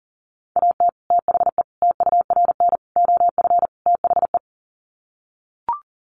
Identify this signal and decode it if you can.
Morse code, heard as a keyed tone, about 40 wpm, 700 Hz. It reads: AT THE TURN OF THE
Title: Sets of 5 Words 40wpm